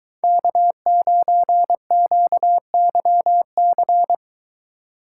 Morse code, heard as a keyed tone, about 21 wpm, 700 Hz. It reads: K9QYC